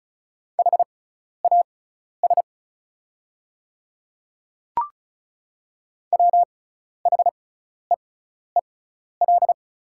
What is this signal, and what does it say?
Morse code, heard as a keyed tone, about 34 wpm, spaced as for 12 wpm, 690 Hz. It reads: HAS WHEEL
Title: 6x Character Spacing - Top 500 Words - 35wpm